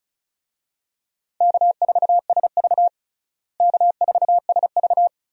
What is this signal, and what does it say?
Morse code, heard as a keyed tone, about 34 wpm, 700 Hz. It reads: K4SV K4SV
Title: USA Call Signs - 4 Characters - Repeated 2x - 35wpm